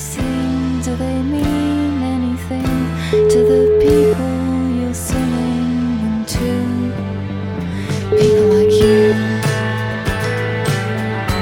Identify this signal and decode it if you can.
Ringback tone (Europe, 425 Hz), with noise.